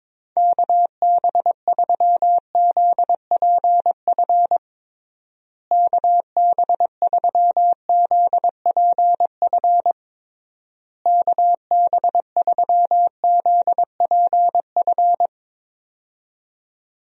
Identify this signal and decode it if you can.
Morse code, heard as a keyed tone, about 22 wpm, 700 Hz. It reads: KB3ZPF KB3ZPF KB3ZPF
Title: USA Call Signs - 6 Characters - Repeated 3x - 22wpm